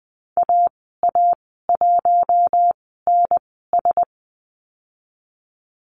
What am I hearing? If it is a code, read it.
Morse code, heard as a keyed tone, about 20 wpm, spaced as for 14 wpm, 700 Hz. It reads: AA1NS
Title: 042 Lesson 8g - Number 1 - Call Signs 20wpm